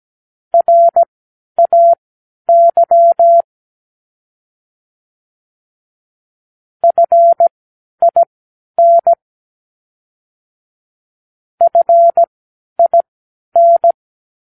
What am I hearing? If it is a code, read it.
Morse code, heard as a keyed tone, about 17 wpm, spaced as for 10 wpm, 680 Hz. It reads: RAY FIN FIN
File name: k7qo / Practicewords